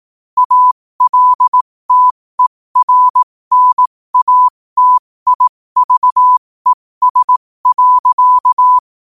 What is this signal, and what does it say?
Morse code, heard as a keyed tone, about 18 wpm, spaced as for 15 wpm, 1000 Hz. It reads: ALTERNATIVES.
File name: qotdfiles / 15wpm